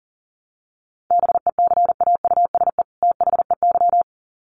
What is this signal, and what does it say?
Morse code, heard as a keyed tone, about 40 wpm, 700 Hz. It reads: BECAUSE THEY
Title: Sets of 2 Words 40wpm